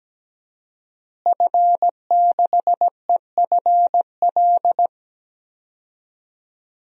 Morse code, heard as a keyed tone, about 17 wpm, 700 Hz. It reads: F6EFL
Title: DXCC-100 Call Signs - 5 Characters - Repeated 2x - 17wpm